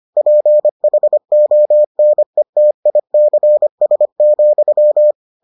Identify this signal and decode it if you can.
Morse code, heard as a keyed tone, about 25 wpm, 600 Hz. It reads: PHONETICS,